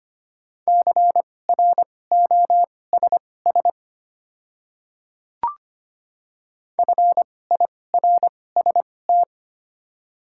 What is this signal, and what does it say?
Morse code, heard as a keyed tone, about 25 wpm, spaced as for 18 wpm, 700 Hz. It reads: CROSS FIRST
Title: 2x Character Spacing - Top 500 Words - 25wpm